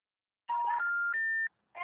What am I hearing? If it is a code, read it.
Special information tone.
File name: out- -2009- - - .121